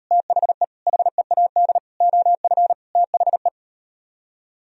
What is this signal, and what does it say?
Morse code, heard as a keyed tone, about 39 wpm, 700 Hz. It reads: THE HEAD OF THE